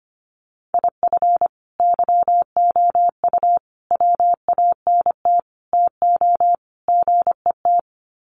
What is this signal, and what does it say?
Morse code, heard as a keyed tone, about 25 wpm, 700 Hz. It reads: IF YOU WANT TO GET
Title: Sets of 5 Words Encore 25wpm